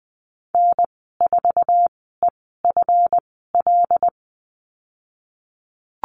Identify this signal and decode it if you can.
Morse code, heard as a keyed tone, about 19 wpm, spaced as for 14 wpm, 700 Hz. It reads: N4EFL
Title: 154 Lesson 25g - Letter p - Call Signs 20wpm